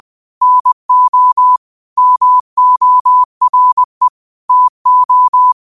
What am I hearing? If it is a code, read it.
Morse code, heard as a keyed tone, about 20 wpm, 1000 Hz. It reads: NO MORE TO
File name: qotdfiles / 20wpm